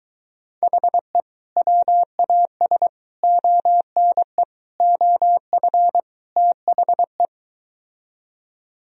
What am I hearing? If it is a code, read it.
Morse code, heard as a keyed tone, about 23 wpm, 700 Hz. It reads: HE WAS ONE OF THE